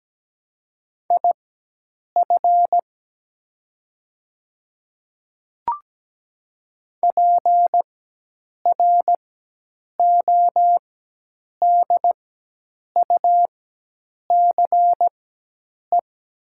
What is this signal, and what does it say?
Morse code, heard as a keyed tone, about 17 wpm, spaced as for 8 wpm, 700 Hz. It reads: IF PRODUCE
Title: 4x Character Spacing - Top 500 Words - 17wpm